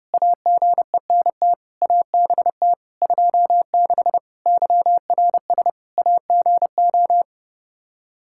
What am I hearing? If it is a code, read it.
Morse code, heard as a keyed tone, about 30 wpm, 700 Hz. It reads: AGENT ABT 26 YRS AGO
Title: Practice QSO - Exchange 3 - More Info - 30wpm